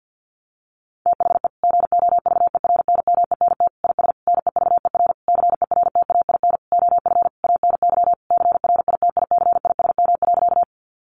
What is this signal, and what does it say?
Morse code, heard as a keyed tone, about 48 wpm, 700 Hz. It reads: THE GOVERNMENT IS NEVER CERTAIN OF ANY CRITICISM.
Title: Sentences from Top 1300 Words 50wpm